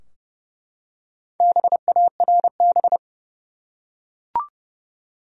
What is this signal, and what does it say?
Morse code, heard as a keyed tone, about 32 wpm, 700 Hz. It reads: BARB
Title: USA Common Names - Max 5 Letters - 30wpm